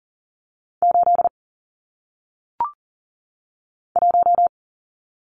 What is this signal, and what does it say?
Morse code, heard as a keyed tone, about 39 wpm, 700 Hz. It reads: 8 1
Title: Numbers - 1-Digit 40wpm